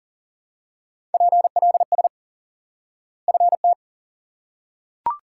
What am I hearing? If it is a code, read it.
Morse code, heard as a keyed tone, about 40 wpm, 700 Hz. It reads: PLS FT
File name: IQE Course 3 - Lesson 89 - YAGI - Two Elements - 40wpm